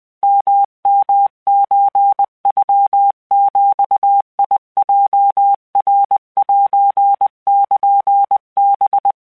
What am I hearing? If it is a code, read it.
Morse code (Japanese Wabun code), heard as a keyed tone, about 20 wpm, 800 Hz. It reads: ヨヨソノビヲナセルハ
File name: keuikuchiyokugo20wptan